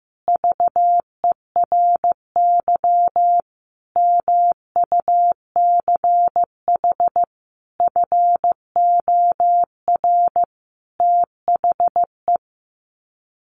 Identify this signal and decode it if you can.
Morse code, heard as a keyed tone, about 15 wpm, 700 Hz. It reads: VERY MUCH FOR THE
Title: Sets of 5 Words Encore2 15wpm